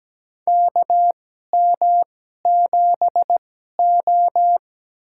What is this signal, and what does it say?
Morse code, heard as a keyed tone, about 17 wpm, spaced as for 12 wpm, 700 Hz. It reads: KM7O
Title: USA Call Signs - 2x Character Spacing - 17wpm